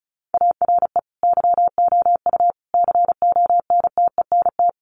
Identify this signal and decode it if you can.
Morse code, heard as a keyed tone, about 34 wpm, 700 Hz. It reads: ARE YOU CONTENT
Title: Sentences from Top 900 Words 35wpm